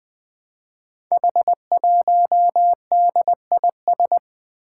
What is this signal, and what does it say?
Morse code, heard as a keyed tone, about 20 wpm, 700 Hz. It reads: H1DIS
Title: Five Challenging Letter-Number Combinations 20wpm